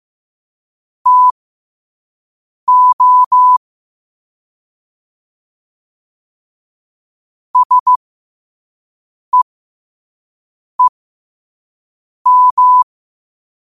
Morse code, heard as a keyed tone, about 15 wpm, spaced as for 5 wpm, 1000 Hz. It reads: TO SEEM
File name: qotdfiles / 5wpm